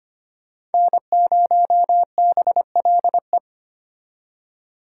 Morse code, heard as a keyed tone, about 33 wpm, 700 Hz. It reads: N0BLE